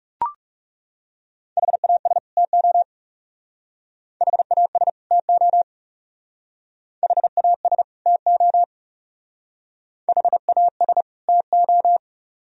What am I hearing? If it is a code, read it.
Morse code, heard as a keyed tone, about 36 wpm, 700 Hz. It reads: HAS TO HAS TO HAS TO HAS TO
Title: Speed-Racing - Sets of 2 Words 30wpm